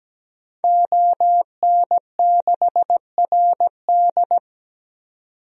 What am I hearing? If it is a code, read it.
Morse code, heard as a keyed tone, about 17 wpm, 700 Hz. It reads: ON6RD
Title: DXCC-100 Call Signs - 5 Characters - 17wpm